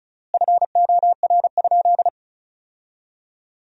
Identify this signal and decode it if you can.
Morse code, heard as a keyed tone, about 36 wpm, 700 Hz. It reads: FOR?
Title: 4x Word-Spacing - Sentences from Top 200 Words - 35wpm